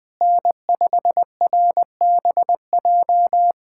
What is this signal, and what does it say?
Morse code, heard as a keyed tone, about 20 wpm, 700 Hz. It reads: N5RBJ
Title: USA Callsign DE Callsign 20wpm